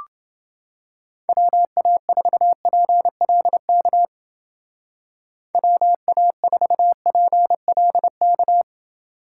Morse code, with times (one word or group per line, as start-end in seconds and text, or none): 1.3-4.0 WA4PLK
5.5-8.6 WA4PLK